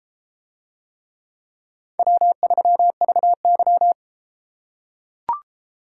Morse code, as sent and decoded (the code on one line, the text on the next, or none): .-- ...-- ...- -.--
W3VY